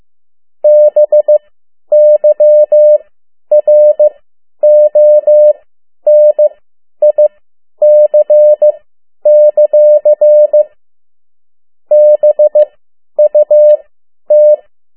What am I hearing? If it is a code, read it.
Morse code, heard as a keyed tone, about 15 wpm, spaced as for 10 wpm, 600 Hz. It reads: BYRONIC; BUT
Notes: Morse code 10 WPM